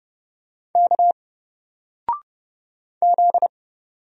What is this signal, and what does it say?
Morse code, heard as a keyed tone, about 30 wpm, 700 Hz. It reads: K Z
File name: Single-Letter-Number-30wpm